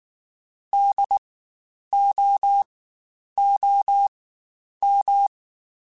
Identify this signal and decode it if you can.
Morse code, heard as a keyed tone, about 19 wpm, spaced as for 9 wpm, 800 Hz. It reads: DOOM
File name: Spaced Alphas / Spaced Alphas mp3s / Ds sp @18